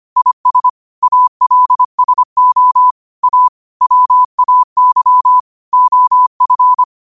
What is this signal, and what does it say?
Morse code, heard as a keyed tone, about 25 wpm, 1000 Hz. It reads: IS ALSO A WAY OF